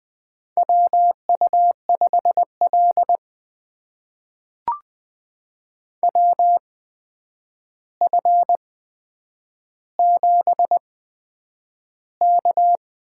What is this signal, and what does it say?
Morse code, heard as a keyed tone, about 20 wpm, spaced as for 9 wpm, 700 Hz. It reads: WU5L WF7K
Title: USA Call Signs - 4 Characters - 8x Character Spacing - 20wpm